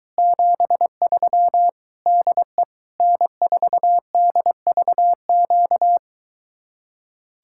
Morse code, heard as a keyed tone, about 23 wpm, 700 Hz. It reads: 73 DE N4DVQ